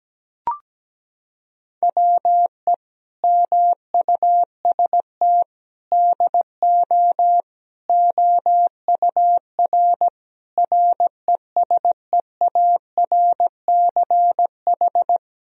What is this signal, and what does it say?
Morse code, heard as a keyed tone, about 17 wpm, 700 Hz. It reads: WE MUST DO OUR RESEARCH
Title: Sentences from Top 1000 Words - Review Part 4 - 17wpm